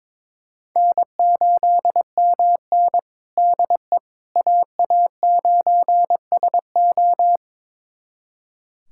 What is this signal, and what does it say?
Morse code, heard as a keyed tone, about 22 wpm, 700 Hz. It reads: N8MN DE AA9SO
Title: USA Callsign DE Callsign 22wpm